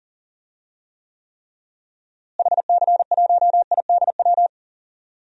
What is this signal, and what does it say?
Morse code, heard as a keyed tone, about 40 wpm, 700 Hz. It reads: HC1IDW
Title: DXCC-100 Call Signs - 6 Characters - Repeated 3x - 40wpm